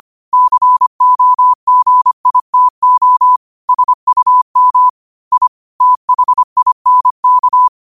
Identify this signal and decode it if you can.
Morse code, heard as a keyed tone, about 25 wpm, 1000 Hz. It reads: COGITO SUM I THINK
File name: qotdfiles / 25wpm